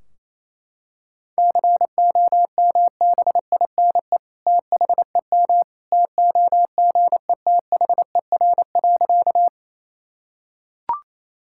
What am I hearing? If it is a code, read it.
Morse code, heard as a keyed tone, about 28 wpm, 700 Hz. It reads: COMBINE THEM TOGETHER.
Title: Sentences from Top 1600 Words 28wpm